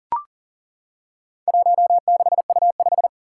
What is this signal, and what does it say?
Morse code, heard as a keyed tone, about 40 wpm, 700 Hz. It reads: 16U5
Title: Four Challenging Letter-Number Combinations 40wpm